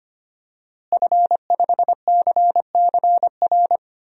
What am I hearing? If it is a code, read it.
Morse code, heard as a keyed tone, about 25 wpm, 700 Hz. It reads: F5CCR
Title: DXCC-100 Call Signs - 5 Characters - Repeated 3x - 25wpm